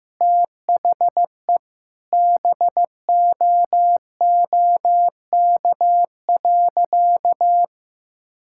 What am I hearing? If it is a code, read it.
Morse code, heard as a keyed tone, about 16 wpm, 700 Hz. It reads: THE BOOK.
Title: Sentences from Top 1000 Words - Review Part 5 - 15wpm